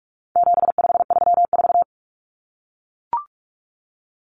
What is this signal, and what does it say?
Morse code, heard as a keyed tone, about 44 wpm, 700 Hz. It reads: 7534
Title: Common Cut Numbers - 4-Digit 45wpm